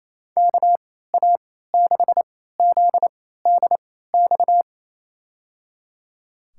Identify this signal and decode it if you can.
Morse code, heard as a keyed tone, about 28 wpm, spaced as for 16 wpm, 700 Hz. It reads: KA6ZDX
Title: USA Call Signs - 6 Characters - 3x Character Spacing - 28wpm